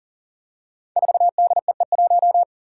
Morse code, heard as a keyed tone, about 40 wpm, 700 Hz. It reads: 4DEE1
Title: Five Challenging Letter-Number Combinations 40wpm